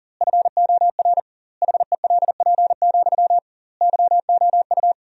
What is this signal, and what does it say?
Morse code, heard as a keyed tone, about 37 wpm, 700 Hz. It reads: FOR HELP, YOU
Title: 2x Word-Spacing - Sentences from Top 300 Words - 40wpm